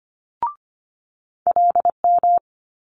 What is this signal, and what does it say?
Morse code, heard as a keyed tone, about 25 wpm, 700 Hz. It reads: LM
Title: Multi-letter Phonemes Quick Fire 25wpm